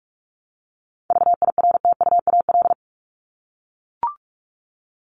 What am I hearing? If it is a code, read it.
Morse code, heard as a keyed tone, about 44 wpm, 700 Hz. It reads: VIRTUAL
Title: LY Suffix - Top 100 Words with Base Words 45wpm